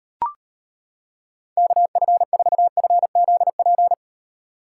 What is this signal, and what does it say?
Morse code, heard as a keyed tone, about 32 wpm, 700 Hz. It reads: KF4FZP